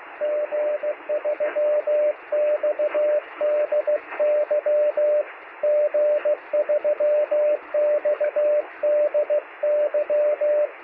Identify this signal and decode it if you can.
Morse code, heard as a keyed tone, about 20 wpm, 600 Hz, with radio noise. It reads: G3XDY G3XDY
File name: sm7lcb / 2115z g3xdy 23cm